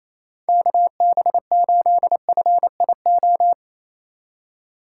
Morse code, heard as a keyed tone, about 28 wpm, 700 Hz. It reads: KB8FIO
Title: USA Call Signs - 6 Characters - 2x Character Spacing - 28wpm